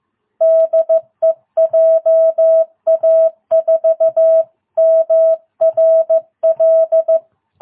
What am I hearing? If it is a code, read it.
Morse code, heard as a keyed tone, about 14 wpm, 660 Hz. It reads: DE JA4MRL
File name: jh8cbh / 2022cw